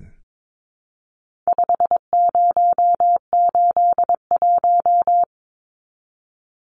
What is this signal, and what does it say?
Morse code, heard as a keyed tone, about 22 wpm, 700 Hz. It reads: 5081
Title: Numbers - 4-Digit 22wpm